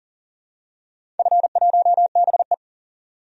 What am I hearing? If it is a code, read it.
Morse code, heard as a keyed tone, about 44 wpm, 690 Hz. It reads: F1BE
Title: DXCC-100 Call Signs - 4 Characters - 40wpm